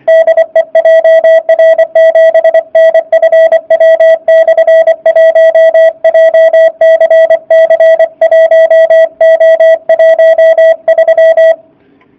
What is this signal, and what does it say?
Morse code, heard as a keyed tone, about 24 wpm, 660 Hz. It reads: DE JR7NFW/1 JCC1O13
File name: jh8cbh / 2019cw / sp29- w